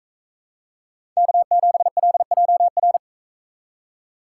Morse code, heard as a keyed tone, about 42 wpm, 700 Hz. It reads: K7LJR